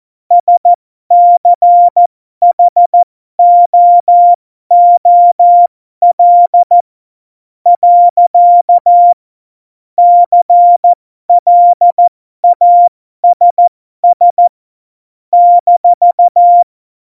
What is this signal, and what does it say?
Morse code, heard as a keyed tone, about 14 wpm, spaced as for 12 wpm, 700 Hz. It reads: SCHOOL . CLASS -
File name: Morsekurs / Witze